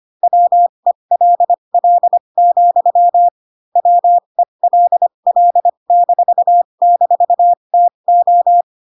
Morse code, heard as a keyed tone, about 25 wpm, spaced as for 22 wpm, 700 Hz. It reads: WELL, WELL--TO